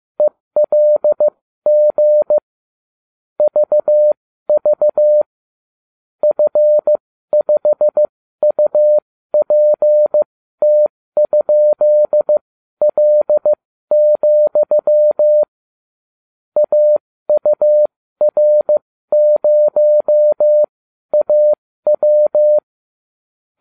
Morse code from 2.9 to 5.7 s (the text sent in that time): VV